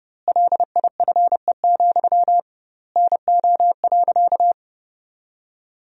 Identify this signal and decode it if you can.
Morse code, heard as a keyed tone, about 30 wpm, 700 Hz. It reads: LIFE, NO.
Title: 2x Word-Spacing - Sentences from Top 300 Words - 30wpm